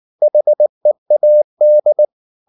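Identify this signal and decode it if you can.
Morse code, heard as a keyed tone, about 19 wpm, 600 Hz. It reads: HEAD
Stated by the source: Morse code at 19 wpm from CW med Gnister, reading Ken Follett's novel Never in English.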